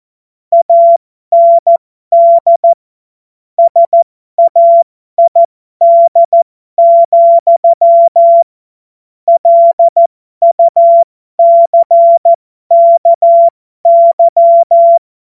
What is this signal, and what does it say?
Morse code, heard as a keyed tone, about 14 wpm, spaced as for 12 wpm, 680 Hz. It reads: AND SAID, LUCKY